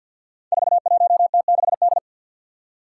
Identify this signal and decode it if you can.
Morse code, heard as a keyed tone, about 52 wpm, 700 Hz. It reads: 41T6D
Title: Five Challenging Letter-Number Combinations 50wpm